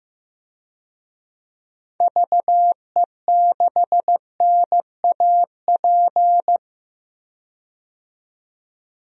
Morse code, heard as a keyed tone, about 15 wpm, 700 Hz. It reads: VE6NAP
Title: DXCC-100 Call Signs - 6 Characters - 15wpm